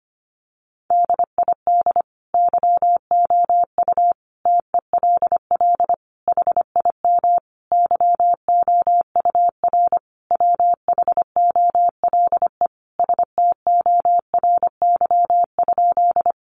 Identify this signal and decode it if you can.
Morse code, heard as a keyed tone, about 25 wpm, 700 Hz. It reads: DID YOU TELL HIM YOUR WHOLE STORY?
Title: Sentences from Top 400 Words 25wpm